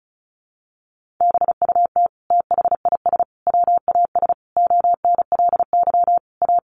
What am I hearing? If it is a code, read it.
Morse code, heard as a keyed tone, about 34 wpm, 700 Hz. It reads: BUT THIS WAS ONLY A
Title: Sentences from Top 1500 Words 35wpm